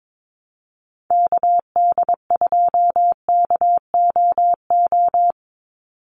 Morse code, heard as a keyed tone, about 22 wpm, 700 Hz. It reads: KD2KOO
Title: USA Call Signs 22wpm